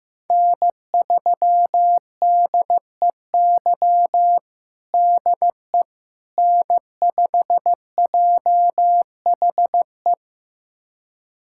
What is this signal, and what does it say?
Morse code, heard as a keyed tone, about 15 wpm, 700 Hz. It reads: N3DEY DE N5JHE
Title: USA Callsign DE Callsign - 5 Characters - 15wpm